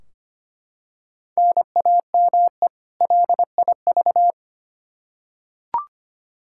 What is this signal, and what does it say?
Morse code, heard as a keyed tone, about 25 wpm, 700 Hz. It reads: NAME LIV
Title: Practice QSO - Exchange 1 - Beginning - 25wpm